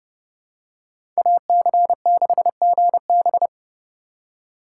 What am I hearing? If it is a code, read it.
Morse code, heard as a keyed tone, about 30 wpm, 700 Hz. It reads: AC6GB
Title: USA Call Signs - 5 Characters - 2x Character Spacing - 30wpm